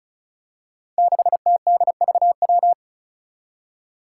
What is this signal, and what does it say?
Morse code, heard as a keyed tone, about 36 wpm, 700 Hz. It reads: 6TDVW
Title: Five Challenging Letter-Number Combinations 35wpm